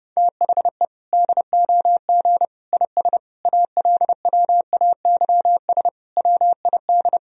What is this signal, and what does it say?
Morse code, heard as a keyed tone, about 28 wpm, 700 Hz. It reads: THE DOG IS ALWAYS WID
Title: Sentences from Top 400 Words and Abbreviations 30wpm